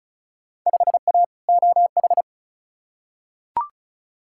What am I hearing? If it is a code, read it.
Morse code, heard as a keyed tone, about 36 wpm, 700 Hz. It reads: 5A OH
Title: Field Day Exchanges Encore - 35wpm